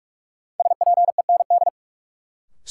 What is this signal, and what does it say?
Morse code, heard as a keyed tone, about 45 wpm, 700 Hz. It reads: SPEND
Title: Sentences from Top 800 Words - New Word Drill - 45wpm